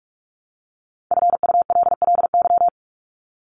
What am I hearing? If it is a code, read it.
Morse code, heard as a keyed tone, about 44 wpm, 700 Hz. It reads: FULLY
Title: LY Suffix - Top 100 Words with Base Words 45wpm